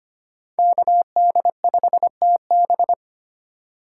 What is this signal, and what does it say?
Morse code, heard as a keyed tone, about 25 wpm, 700 Hz. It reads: KD5TB
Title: USA Call Signs - 5 Characters - 4x Character Spacing - 25wpm